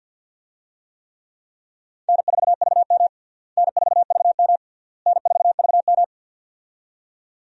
Morse code, heard as a keyed tone, about 53 wpm, 700 Hz. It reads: N4VM N4VM N4VM
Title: USA Call Signs - 4 Characters - Repeated 3x - 50wpm